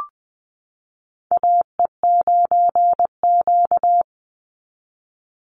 Morse code, heard as a keyed tone, about 20 wpm, 700 Hz. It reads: AE9Q
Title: NA SOTA Chaser Callsigns 20wpm